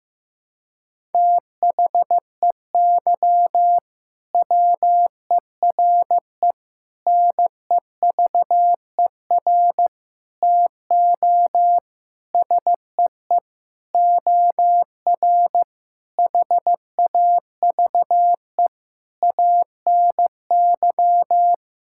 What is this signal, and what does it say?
Morse code, heard as a keyed tone, about 15 wpm, 700 Hz. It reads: THEY WERE NEVER TO SEE OR HAVE ANY
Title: Sentences from Top 1000 Words - Review Part 6 - 15wpm